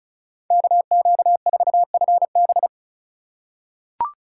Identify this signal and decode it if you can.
Morse code, heard as a keyed tone, about 36 wpm, 700 Hz. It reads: KQ4FB
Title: USA Call Signs - 5 Characters - 2x Character Spacing - 35wpm